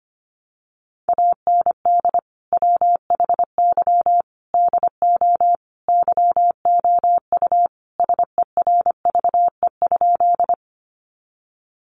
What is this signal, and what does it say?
Morse code, heard as a keyed tone, about 25 wpm, 700 Hz. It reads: AND WHY DO YOU SERVE?
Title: Sentences from Top 400 Words 25wpm